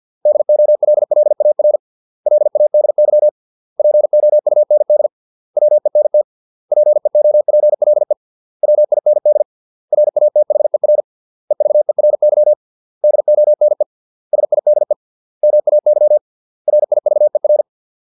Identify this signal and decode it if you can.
Morse code, heard as a keyed tone, about 50 wpm, 600 Hz. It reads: DOLLAR LADY POUND WENT PEOPLE WIND RATHER EVERY DONE SIDE MAY RIVER